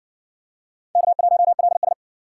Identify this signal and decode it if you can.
Morse code, heard as a keyed tone, about 60 wpm, 700 Hz. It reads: K1LS